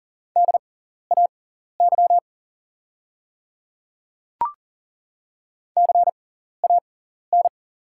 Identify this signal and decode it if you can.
Morse code, heard as a keyed tone, about 42 wpm, spaced as for 14 wpm, 690 Hz. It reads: DAY CAN
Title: 6x Character Spacing - Top 200 Words - 40wpm